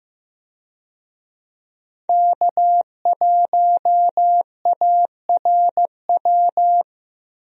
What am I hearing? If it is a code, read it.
Morse code, heard as a keyed tone, about 15 wpm, 700 Hz. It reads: K1ARW
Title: USA Call Signs - 5 Characters - Repeated 3x - 15wpm